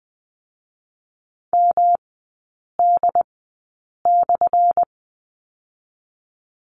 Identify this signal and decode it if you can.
Morse code, heard as a keyed tone, about 20 wpm, 700 Hz. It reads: M D /
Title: 178 Lesson 29e - Slash - Three Characters 20wpm